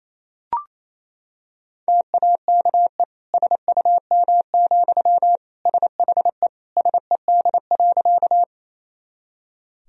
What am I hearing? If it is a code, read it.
Morse code, heard as a keyed tone, about 28 wpm, 700 Hz. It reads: TAKE SUM, SHE SED.
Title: Sentences from Top 200 Words and Abbreviations 28wpm